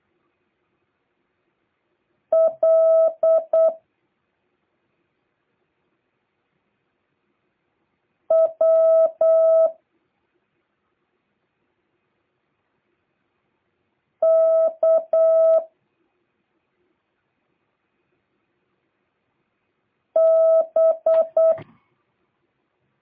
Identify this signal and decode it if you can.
Morse code, heard as a keyed tone, about 8 wpm, 650 Hz. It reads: L W K B